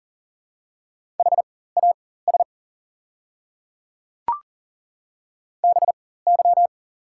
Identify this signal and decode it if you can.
Morse code, heard as a keyed tone, about 42 wpm, spaced as for 19 wpm, 690 Hz. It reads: HAS BY
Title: 4x Character Spacing - Top 100 Words - 40wpm